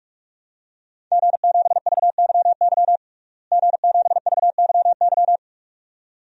Morse code, heard as a keyed tone, about 45 wpm, 700 Hz. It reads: G7VYY G7VYY
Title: DXCC-100 Call Signs - 5 Characters - Repeated 2x - 45wpm